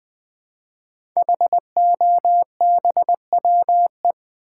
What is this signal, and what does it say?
Morse code, heard as a keyed tone, about 20 wpm, 700 Hz. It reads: HOBWE
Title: Code Groups - 5 Letters - 20wpm